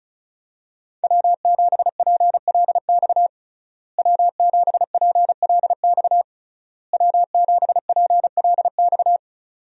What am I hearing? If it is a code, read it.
Morse code, heard as a keyed tone, about 36 wpm, 700 Hz. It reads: W7PLX W7PLX W7PLX
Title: USA Call Signs - 5 Characters - Repeated 3x - 35wpm